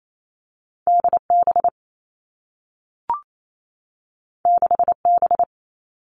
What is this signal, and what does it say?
Morse code, heard as a keyed tone, about 28 wpm, 700 Hz. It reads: D B 6 B
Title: Two Challenging Letter-Number Combinations 28wpm